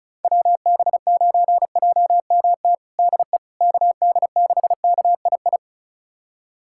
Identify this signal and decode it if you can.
Morse code, heard as a keyed tone, about 36 wpm, 700 Hz. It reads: WB9JMT DE KD6KII
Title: USA Callsign DE Callsign - 6 Characters - 35wpm